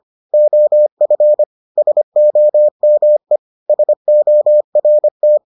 Morse code, heard as a keyed tone, about 25 wpm, 600 Hz. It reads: OF SOME SORT